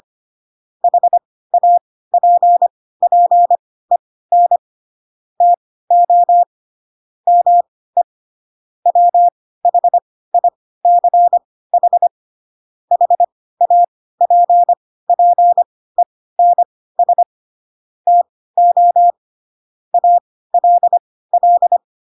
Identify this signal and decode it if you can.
Morse code, heard as a keyed tone, about 25 wpm, spaced as for 16 wpm, 700 Hz. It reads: HAPPEN TO ME WHICH HAPPENS TO ALL
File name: Pinocchio / Pinocchio ch4 10 25 ik1hge